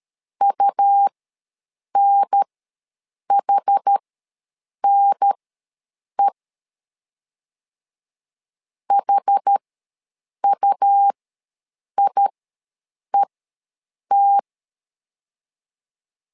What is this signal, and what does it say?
Morse code, heard as a keyed tone, about 13 wpm, spaced as for 7 wpm, 800 Hz. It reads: UNHNE HUIET
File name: LEZIONI / 2 UHDN / 2 UHDN ws 12wpm 1708